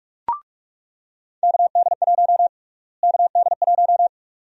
Morse code, heard as a keyed tone, about 47 wpm, 700 Hz. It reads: KD1 KD1
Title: USA Call Sign Prefixes - Repeated 2x - 45wpm